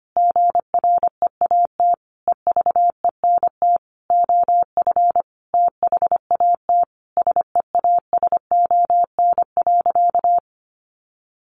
Morse code, heard as a keyed tone, about 25 wpm, 700 Hz. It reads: GREAT EVENT OF THAT SEASON.
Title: Sentences from Top 1400 Words 25wpm